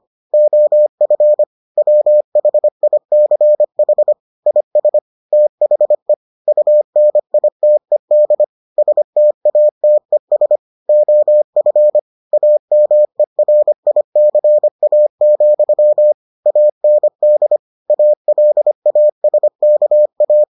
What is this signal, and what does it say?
Morse code, heard as a keyed tone, about 25 wpm, 600 Hz. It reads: OF WHICH IS THE UNITED STATES OF AMERICA, AND ALASKA